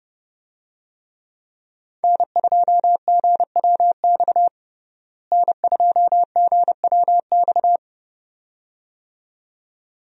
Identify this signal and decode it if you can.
Morse code, heard as a keyed tone, about 30 wpm, 700 Hz. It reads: N2GWX N2GWX
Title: USA Call Signs - 5 Characters - Repeated 2x - 30wpm